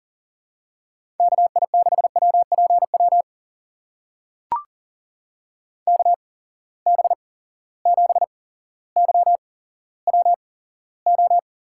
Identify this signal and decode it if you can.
Morse code, heard as a keyed tone, about 40 wpm, spaced as for 17 wpm, 700 Hz. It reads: KI6WPW KB7YWO
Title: USA Call Signs - 6 Characters - 8x Character Spacing - 40wpm